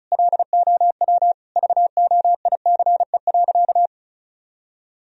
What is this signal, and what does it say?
Morse code, heard as a keyed tone, about 34 wpm, 700 Hz. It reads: LOW VOICE.
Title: Sentences from Top 1000 Words - Review Part 1 - 35wpm